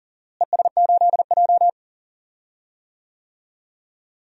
Morse code, heard as a keyed tone, about 40 wpm, 700 Hz. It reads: ES8J
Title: DXCC-100 Call Signs - 4 Characters - Repeated 2x - 40wpm